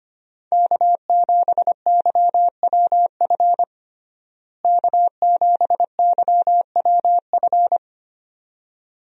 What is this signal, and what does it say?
Morse code, heard as a keyed tone, about 25 wpm, 700 Hz. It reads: K7YWF K7YWF
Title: USA Call Signs - 5 Characters - Repeated 2x - 25wpm